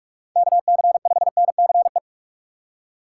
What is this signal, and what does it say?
Morse code, heard as a keyed tone, about 47 wpm, 700 Hz. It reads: KC5NCE
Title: USA Call Signs - 6 Characters - 3x Character Spacing - 45wpm